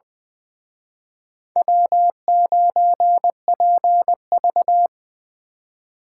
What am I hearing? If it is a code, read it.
Morse code, heard as a keyed tone, about 25 wpm, 700 Hz. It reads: W9PV
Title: Speed Racing - USA 1x2 Call Signs 20wpm